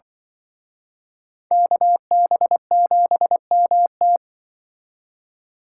Morse code, heard as a keyed tone, about 24 wpm, 700 Hz. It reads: KB7MT
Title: Speed Racing - USA Call Signs 20wpm